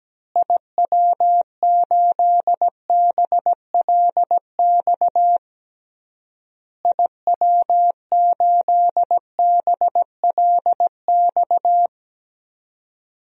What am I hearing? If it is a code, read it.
Morse code, heard as a keyed tone, about 17 wpm, 700 Hz. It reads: IW8BLX IW8BLX
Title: DXCC-100 Call Signs - 6 Characters - Repeated 3x - 17wpm